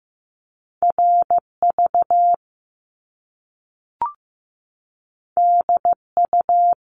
Morse code, heard as a keyed tone, about 15 wpm, 700 Hz. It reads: RV DU
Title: Top Two Letter Combinations in Words 15wpm